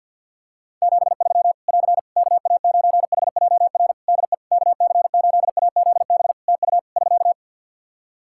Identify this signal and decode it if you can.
Morse code, heard as a keyed tone, about 49 wpm, 700 Hz. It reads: 73 <AR> KA9HJR DE KC8AZB TU <SK>
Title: Practice QSO - Exchange 4 - Ending - 50wpm